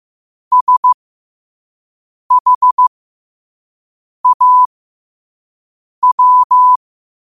Morse code, heard as a keyed tone, about 15 wpm, spaced as for 5 wpm, 1000 Hz. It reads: SHAW
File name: qotdfiles / 5wpm